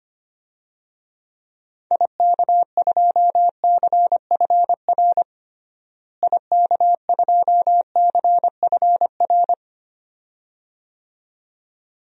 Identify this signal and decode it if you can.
Morse code, heard as a keyed tone, about 25 wpm, 700 Hz. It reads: IK2CFR IK2CFR
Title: DXCC-100 Call Signs - 6 Characters - Repeated 2x - 25wpm